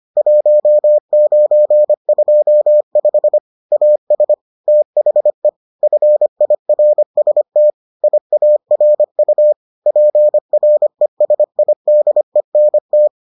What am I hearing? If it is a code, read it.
Morse code, heard as a keyed tone, about 25 wpm, 600 Hz. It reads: 1925 AS THE FIRST IARU PRESIDENT